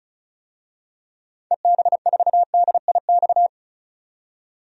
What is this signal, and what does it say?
Morse code, heard as a keyed tone, about 34 wpm, 700 Hz. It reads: EB4DIX
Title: DXCC-100 Call Signs - 6 Characters - 35wpm